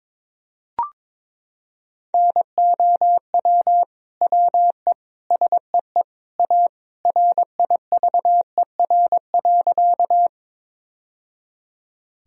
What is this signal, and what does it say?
Morse code, heard as a keyed tone, about 22 wpm, 700 Hz. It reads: NOW WE SEE A RIVER.
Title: Sentences from Top 1000 Words - Review Part 1 - 22wpm